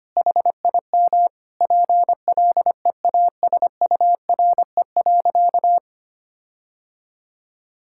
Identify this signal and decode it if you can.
Morse code, heard as a keyed tone, about 25 wpm, 700 Hz. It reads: HIM PLEASURE.
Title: Sentences from Top 1600 Words 25wpm